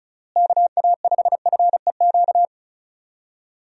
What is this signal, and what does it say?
Morse code, heard as a keyed tone, about 34 wpm, 700 Hz. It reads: KA5FEQ
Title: USA Call Signs - 6 Characters - Repeated 3x - 35wpm